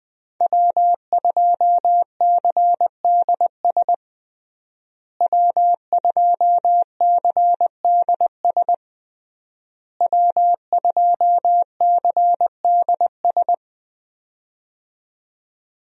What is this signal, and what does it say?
Morse code, heard as a keyed tone, about 21 wpm, 700 Hz. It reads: W2CDS W2CDS W2CDS
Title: USA Call Signs - 5 Characters - Repeated 3x - 20wpm